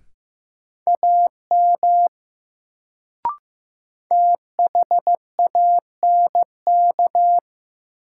Morse code, heard as a keyed tone, about 16 wpm, 700 Hz. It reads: AM THANK
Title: Sentences from Top 600 Words - New Word Drill - 15wpm